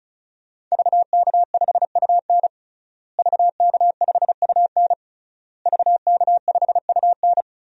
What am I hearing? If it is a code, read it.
Morse code, heard as a keyed tone, about 35 wpm, 700 Hz. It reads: VK5UN VK5UN VK5UN
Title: DXCC-100 Call Signs - 5 Characters - Repeated 3x - 35wpm